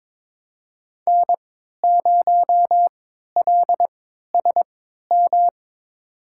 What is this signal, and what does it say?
Morse code, heard as a keyed tone, about 23 wpm, spaced as for 13 wpm, 700 Hz. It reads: N0LSM
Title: USA Call Signs - 3x Character Spacing - 22wpm